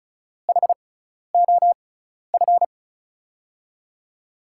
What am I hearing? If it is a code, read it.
Morse code, heard as a keyed tone, about 36 wpm, spaced as for 12 wpm, 700 Hz. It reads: HOF
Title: USA Call Signs - 6x Character Spacing - 35wpm